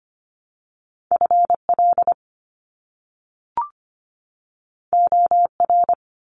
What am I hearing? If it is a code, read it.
Morse code, heard as a keyed tone, about 25 wpm, 700 Hz. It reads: FL OR
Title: US States 25wpm